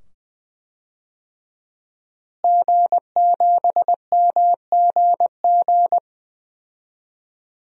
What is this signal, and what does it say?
Morse code, heard as a keyed tone, about 20 wpm, 700 Hz. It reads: G7MGG
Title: DXCC-100 Call Signs - 5 Characters - Repeated 2x - 20wpm